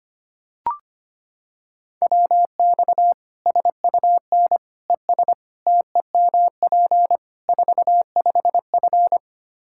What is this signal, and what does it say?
Morse code, heard as a keyed tone, about 25 wpm, 700 Hz. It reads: WX SUN ES TEMP 45F
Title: Practice QSO - Exchange 2 - Info - 25wpm